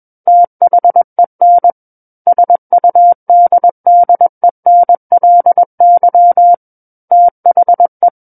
Morse code, heard as a keyed tone, about 21 wpm, 700 Hz. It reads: THEN SUDDENLY THE
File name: Book One / 12-Chapter 12 - 21 WPM